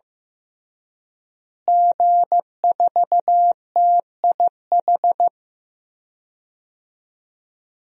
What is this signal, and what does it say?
Morse code, heard as a keyed tone, about 15 wpm, 700 Hz. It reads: G4TIH
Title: DXCC-100 Call Signs - 5 Characters - Repeated 2x - 15wpm